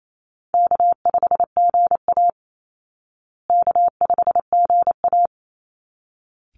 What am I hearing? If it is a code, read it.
Morse code, heard as a keyed tone, about 28 wpm, 700 Hz. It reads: K5GA K5GA
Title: Easy Call Signs Repeated 2x - 28wpm